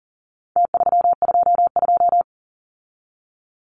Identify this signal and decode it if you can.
Morse code, heard as a keyed tone, about 40 wpm, 700 Hz. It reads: T322
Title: Numbers - Contest Style - 4-Digit 40wpm